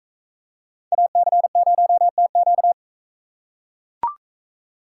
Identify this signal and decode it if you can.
Morse code, heard as a keyed tone, about 42 wpm, 700 Hz. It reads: AC0TQ